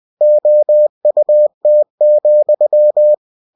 Morse code, heard as a keyed tone, about 20 wpm, 600 Hz. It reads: OUT,
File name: cw-books / winnie-the-pooh / winnie-20wpm-page-0028